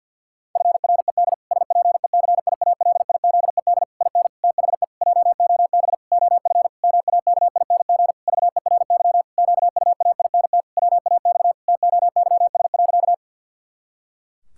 Morse code, heard as a keyed tone, about 52 wpm, 700 Hz. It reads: FRED SPECIALIZED IN THE JOB OF MAKING VERY QUAINT WAX TOYS.
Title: Pangram Challenge v2 - 50wpm